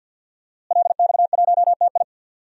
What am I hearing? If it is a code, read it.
Morse code, heard as a keyed tone, about 50 wpm, 700 Hz. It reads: LX1TI
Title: DXCC-100 Call Signs - 5 Characters - Repeated 3x - 50wpm